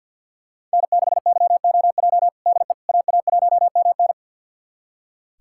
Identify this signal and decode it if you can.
Morse code, heard as a keyed tone, about 46 wpm, 700 Hz. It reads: N6YOJ DE AA1MN
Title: USA Callsign DE Callsign - 5 Characters - 50wpm